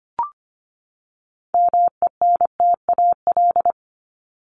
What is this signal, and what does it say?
Morse code, heard as a keyed tone, about 25 wpm, 700 Hz. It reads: MENTAL
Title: Top 501-750 Words 25wpm